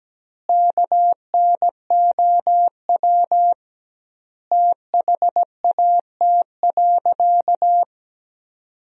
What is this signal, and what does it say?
Morse code, heard as a keyed tone, about 17 wpm, 700 Hz. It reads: KNOW THAT.
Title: 2x Word-Spacing - Sentences from Top 500 Words - 17wpm